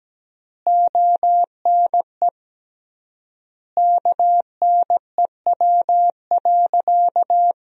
Morse code, heard as a keyed tone, about 17 wpm, 700 Hz. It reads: ONE KNEW.
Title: 3x Word-Spacing - Sentences from Top 400 Words - 17wpm